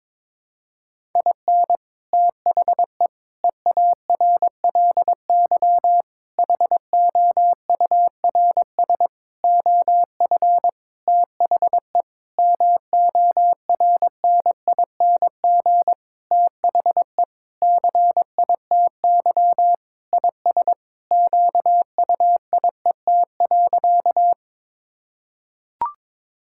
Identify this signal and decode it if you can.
Morse code, heard as a keyed tone, about 21 wpm, 700 Hz. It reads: IN THE EARLY HOURS OF THE MORNING THE CITY IS QUIET.
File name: Sentences with 7-Word Phrases - 22wpm